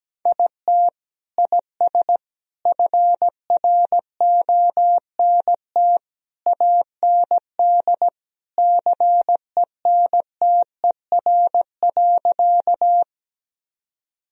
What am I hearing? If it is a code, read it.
Morse code, heard as a keyed tone, about 17 wpm, 700 Hz. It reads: IT IS FRONT AND CENTER.
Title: Sentences from Top 1000 Words - Review Part 2 - 17wpm